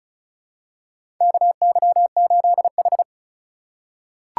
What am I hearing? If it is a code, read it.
Morse code, heard as a keyed tone, about 36 wpm, 700 Hz. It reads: KY8H
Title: USA Call Signs - 4x Character Spacing - 35wpm